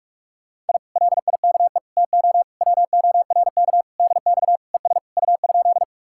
Morse code, heard as a keyed tone, about 63 wpm, spaced as for 53 wpm, 700 Hz. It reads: I LIKE TO WORK DX ES U?
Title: Speed-Racing Common QSO Phrases Encore - 45wpm